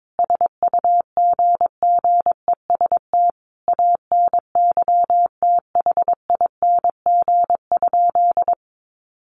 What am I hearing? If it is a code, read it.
Morse code, heard as a keyed tone, about 22 wpm, 700 Hz. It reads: SUGGEST ANYTHING?
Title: Sentences from Top 1300 Words 22wpm